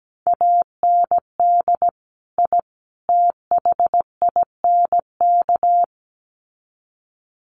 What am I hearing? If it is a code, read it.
Morse code, heard as a keyed tone, about 17 wpm, 700 Hz. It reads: AND I THINK
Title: Sets of 3 Words 17wpm